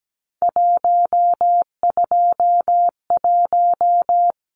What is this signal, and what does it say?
Morse code, heard as a keyed tone, about 17 wpm, 700 Hz. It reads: 121
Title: Numbers - 3-Digit 17wpm